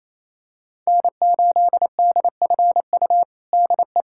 Morse code, heard as a keyed tone, about 28 wpm, 700 Hz. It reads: N8DFU DE
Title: USA Callsign DE Callsign - 5 Characters - 28wpm